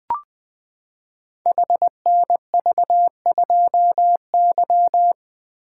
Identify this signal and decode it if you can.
Morse code, heard as a keyed tone, about 20 wpm, 700 Hz. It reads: HNV2Y
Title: Code Groups - 5 Letter:Numbers - 20wpm